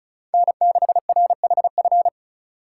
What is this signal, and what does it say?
Morse code, heard as a keyed tone, about 43 wpm, 700 Hz. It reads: N6RHF
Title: Speed Racing - USA 1x3 Call Signs 35wpm